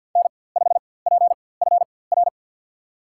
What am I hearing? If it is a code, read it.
Morse code, heard as a keyed tone, about 52 wpm, spaced as for 24 wpm, 700 Hz. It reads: N5PFR
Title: USA Call Signs - 5 Characters - 4x Character Spacing - 50wpm